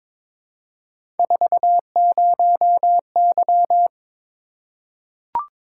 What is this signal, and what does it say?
Morse code, heard as a keyed tone, about 22 wpm, 700 Hz. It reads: 40Y
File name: practice / 22wpm / Three Letter-Number Combinations - 22wpm